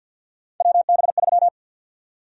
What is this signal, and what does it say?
Morse code, heard as a keyed tone, about 53 wpm, 700 Hz. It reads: WB3
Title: USA Call Sign Prefixes - 50wpm